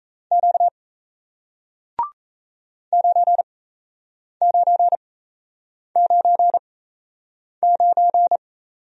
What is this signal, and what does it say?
Morse code, heard as a keyed tone, about 38 wpm, spaced as for 8 wpm, 700 Hz. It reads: Q 9999